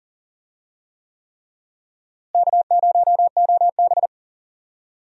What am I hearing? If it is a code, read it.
Morse code, heard as a keyed tone, about 40 wpm, 700 Hz. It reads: K0OB
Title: USA Call Signs - 4 Characters - Repeated 3x - 40wpm